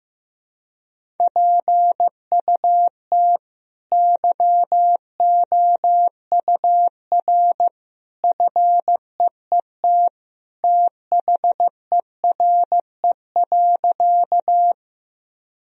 Morse code, heard as a keyed tone, about 15 wpm, 700 Hz. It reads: PUT YOUR FEET THERE.
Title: Sentences from Top 1000 Words - Review Part 1 - 15wpm